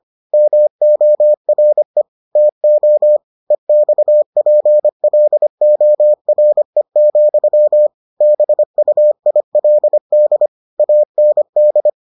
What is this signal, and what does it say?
Morse code, heard as a keyed tone, about 25 wpm, 600 Hz. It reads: MORE TO EXPLORE, BUILD AND